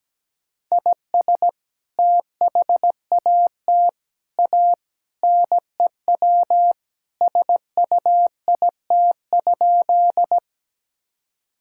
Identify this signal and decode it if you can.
Morse code, heard as a keyed tone, about 17 wpm, 700 Hz. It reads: IS THAT A NEW SUIT?
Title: Sentences from Top 1600 Words 17wpm